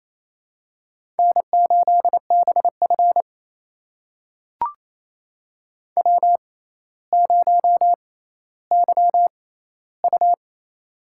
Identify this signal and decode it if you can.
Morse code, heard as a keyed tone, about 28 wpm, spaced as for 14 wpm, 700 Hz. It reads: N8BF W0YU
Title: USA Call Signs - 4 Characters - 6x Character Spacing - 28wpm